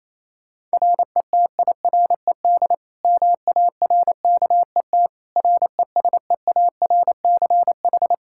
Morse code, heard as a keyed tone, about 27 wpm, 700 Hz. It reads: RETIRED MARKET RESEARCH
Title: Practice QSO - Exchange 3 - More Info - 28wpm